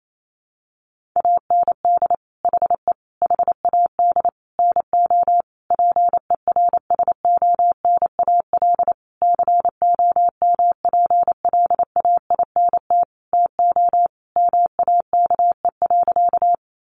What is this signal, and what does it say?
Morse code, heard as a keyed tone, about 28 wpm, 700 Hz. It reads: AND HE HAD NO PERSONAL COMPLAINT TO MAKE.
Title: Sentences from Top 1400 Words 28wpm